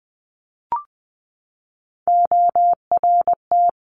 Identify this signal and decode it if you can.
Morse code, heard as a keyed tone, about 20 wpm, 700 Hz. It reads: ORT
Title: Top Three Letter Combinations in Words 20wpm